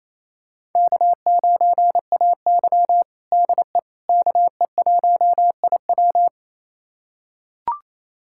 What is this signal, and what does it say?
Morse code, heard as a keyed tone, about 28 wpm, 700 Hz. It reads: K9AY DE KE1IW
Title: USA Callsign DE Callsign 28wpm